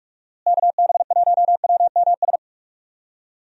Morse code, heard as a keyed tone, about 46 wpm, 700 Hz. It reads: KB1WMS
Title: USA Call Signs - 6 Characters - Repeated 2x - 45wpm